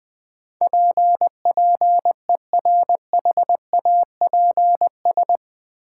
Morse code, heard as a keyed tone, about 20 wpm, 700 Hz. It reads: PPERHAPS
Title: Sentences from Top 1100 Words - New Word Drill Alt - 20wpm